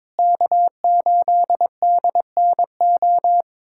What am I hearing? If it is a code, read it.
Morse code, heard as a keyed tone, about 22 wpm, 700 Hz. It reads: K8DNO
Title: USA Callsign DE Callsign 22wpm